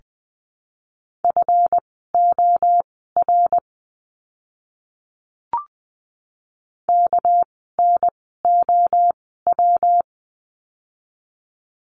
Morse code, heard as a keyed tone, about 20 wpm, spaced as for 14 wpm, 700 Hz. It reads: FOR KNOW
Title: 230 Lesson 37f - Letter x - Words 20wpm